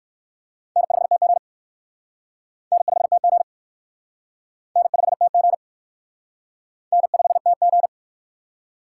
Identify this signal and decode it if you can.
Morse code, heard as a keyed tone, about 56 wpm, 700 Hz. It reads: N5TG N5TG N5TG N5TG